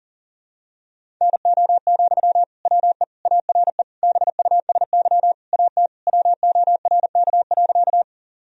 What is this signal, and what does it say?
Morse code, heard as a keyed tone, about 40 wpm, 700 Hz. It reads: NO, WE ARE BUSY AT WORK.
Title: Sentences from Top 1000 Words - Review Part 3 - 40wpm